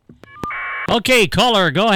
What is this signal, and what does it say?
Caller ID data burst, with speech.